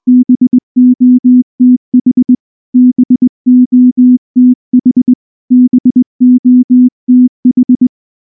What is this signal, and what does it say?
Morse code, heard as a keyed tone, about 19 wpm, 250 Hz. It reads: BOTH BOTH BOTH
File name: morse / morse-training-20WPM-9